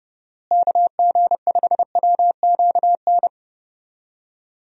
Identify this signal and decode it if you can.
Morse code, heard as a keyed tone, about 30 wpm, 700 Hz. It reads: KG5WQN
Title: USA Call Signs - 6 Characters - Repeated 3x - 30wpm